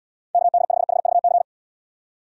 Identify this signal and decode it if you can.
Morse code, heard as a keyed tone, about 70 wpm, 700 Hz. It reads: KD5HVC